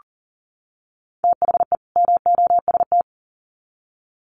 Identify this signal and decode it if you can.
Morse code, heard as a keyed tone, about 40 wpm, 700 Hz. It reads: THE MOST
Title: Sets of 2 Words 40wpm